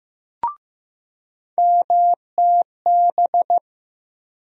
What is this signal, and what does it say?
Morse code, heard as a keyed tone, about 15 wpm, 700 Hz. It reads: MTB
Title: USA Call Sign Postfixes - 3 Letters 15wpm